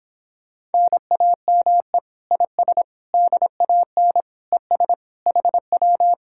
Morse code, heard as a keyed tone, about 31 wpm, 700 Hz. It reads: NAME IS DAN ES HW